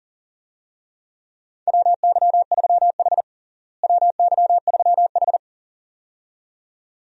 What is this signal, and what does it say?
Morse code, heard as a keyed tone, about 40 wpm, 700 Hz. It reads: WY3H WY3H
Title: USA Call Signs - 4 Characters - Repeated 2x - 40wpm